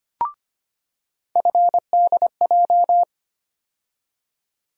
Morse code, heard as a keyed tone, about 25 wpm, 700 Hz. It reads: FDJ
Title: USA Call Sign Postfixes - 3 Letters 25wpm